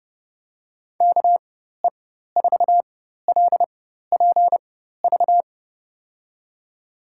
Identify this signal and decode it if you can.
Morse code, heard as a keyed tone, about 30 wpm, spaced as for 14 wpm, 700 Hz. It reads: KE4LPV
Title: USA Call Signs - 6 Characters - 4x Character Spacing - 30wpm